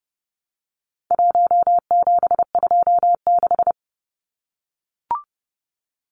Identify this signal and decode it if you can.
Morse code, heard as a keyed tone, about 30 wpm, 700 Hz. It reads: 1726
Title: Numbers - 4-Digit 30wpm